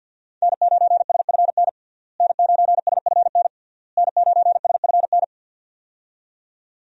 Morse code, heard as a keyed tone, about 49 wpm, 700 Hz. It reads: N9SFN N9SFN N9SFN
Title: USA Call Signs - 5 Characters - Repeated 3x - 50wpm